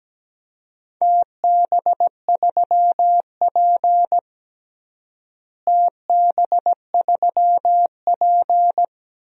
Morse code, heard as a keyed tone, about 17 wpm, 700 Hz. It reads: TB3P TB3P
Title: DXCC-100 Call Signs - 4 Characters - Repeated 3x - 17wpm